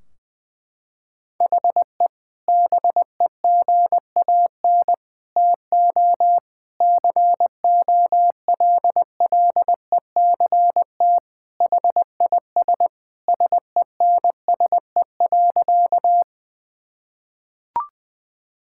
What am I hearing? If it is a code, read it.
Morse code, heard as a keyed tone, about 20 wpm, 700 Hz. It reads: HE BEGAN TO COLLECT HIS SENSE.
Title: Sentences from Top 1600 Words 20wpm